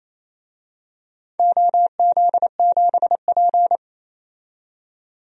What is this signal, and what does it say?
Morse code, heard as a keyed tone, about 28 wpm, 700 Hz. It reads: OZ7P
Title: DXCC-100 Call Signs - 4 Characters - 28wpm